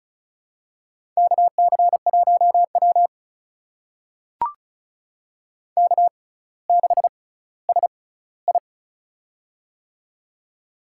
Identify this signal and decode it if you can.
Morse code, heard as a keyed tone, about 35 wpm, spaced as for 18 wpm, 700 Hz. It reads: KC1W K6SI
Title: USA Call Signs - 4 Characters - 6x Character Spacing - 35wpm